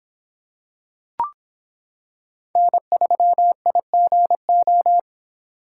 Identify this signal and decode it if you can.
Morse code, heard as a keyed tone, about 26 wpm, 700 Hz. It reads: N3IGO